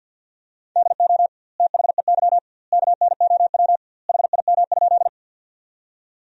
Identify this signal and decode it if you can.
Morse code, heard as a keyed tone, about 53 wpm, 700 Hz. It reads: DO THEY KNOW HIM?
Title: 2x Word-Spacing - Sentences from Top 100 Words - 50wpm